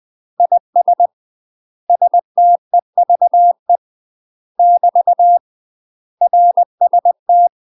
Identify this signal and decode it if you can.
Morse code, heard as a keyed tone, about 20 wpm, 700 Hz. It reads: IS STEVE <BT> RST
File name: UK MASTER QSOs TRAINING / 20WPM / UK MASTER QSOs 0048